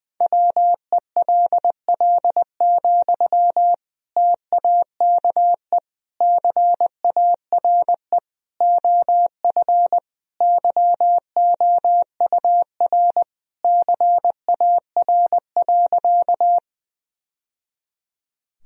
Morse code, heard as a keyed tone, about 20 wpm, 700 Hz. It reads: WELL, TAKE CARE OF YOUR CAR.
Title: Sentences from Top 1000 Words - Review Part 1 - 20wpm